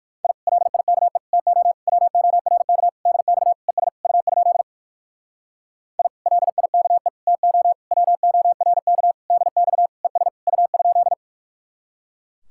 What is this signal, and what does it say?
Morse code, heard as a keyed tone, about 63 wpm, 700 Hz. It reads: I LIKE TO WORK DX ES U? I LIKE TO WORK DX ES U?